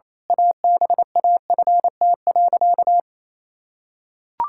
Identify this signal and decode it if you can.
Morse code, heard as a keyed tone, about 28 wpm, 700 Hz. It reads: ABAFT.
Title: Pangram Challenge v2 - 28wpm